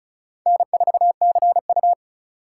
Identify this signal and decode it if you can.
Morse code, heard as a keyed tone, about 34 wpm, 700 Hz. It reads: N4CU
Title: USA Call Signs - 4 Characters - 6x Character Spacing - 35wpm